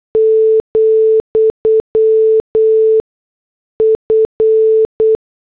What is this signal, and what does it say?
Morse code, heard as a keyed tone, about 8 wpm, 440 Hz. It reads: , F
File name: morse / headlines